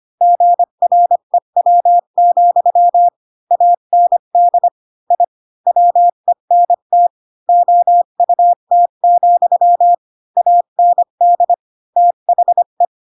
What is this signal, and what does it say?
Morse code, heard as a keyed tone, about 25 wpm, 700 Hz. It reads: GREW, AND I WENT OUT, AND THE
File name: Pinocchio / Pinocchio ch7 20 25 ik1hge